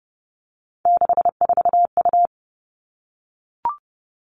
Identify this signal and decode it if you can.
Morse code, heard as a keyed tone, about 30 wpm, 700 Hz. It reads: 64U
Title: Three Challenging Letter-Number Combinations 30wpm